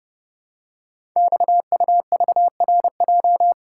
Morse code, heard as a keyed tone, about 29 wpm, 700 Hz. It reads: XUVRJ
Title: Code Groups - 5 Letters - 30wpm